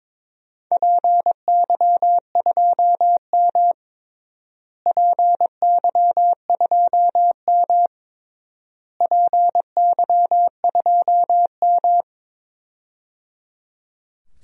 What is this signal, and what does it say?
Morse code, heard as a keyed tone, about 22 wpm, 700 Hz. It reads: PY2M PY2M PY2M
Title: DXCC-100 Call Signs - 4 Characters - Repeated 3x - 22wpm